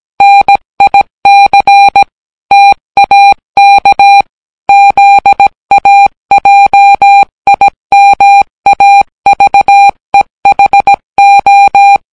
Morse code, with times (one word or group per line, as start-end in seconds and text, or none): 0.2-2.0 NIC
2.5-4.2 TAK
4.7-12.0 ZAJIMAVEHO